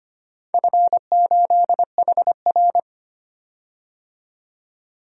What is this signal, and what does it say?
Morse code, heard as a keyed tone, about 25 wpm, 700 Hz. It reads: F8HR
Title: DXCC-100 Call Signs - 4 Characters - 25wpm